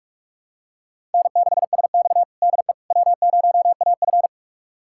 Morse code, heard as a keyed tone, about 41 wpm, 700 Hz. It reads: N6SX DE W0AF
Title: USA Callsign DE Callsign - 4 Characters - 45wpm